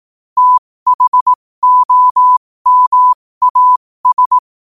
Morse code, heard as a keyed tone, about 18 wpm, spaced as for 15 wpm, 1000 Hz. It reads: THOMAS
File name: qotdfiles / 15wpm